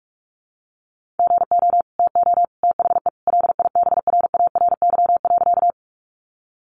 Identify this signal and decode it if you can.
Morse code, heard as a keyed tone, about 44 wpm, 700 Hz. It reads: GO TO THE LIBRARY.
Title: Sentences from Top 600 Words 45wpm